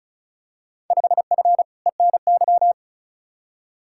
Morse code, heard as a keyed tone, about 36 wpm, 700 Hz. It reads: 5F ENY
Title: Field Day Exchanges Encore - 35wpm